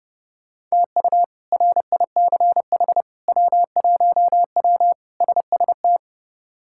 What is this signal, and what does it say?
Morse code, heard as a keyed tone, about 29 wpm, 700 Hz. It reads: TU RICH W1W SST
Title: K1USN SST QSO Practice - 30wpm